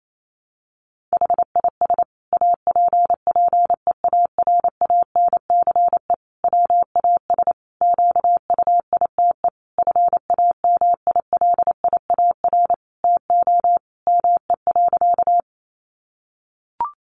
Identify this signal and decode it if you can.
Morse code, heard as a keyed tone, about 28 wpm, 700 Hz. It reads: HIS APPEARANCE WAS QUITE FAMILIAR TO ME.
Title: Sentences from Top 1100 Words 28wpm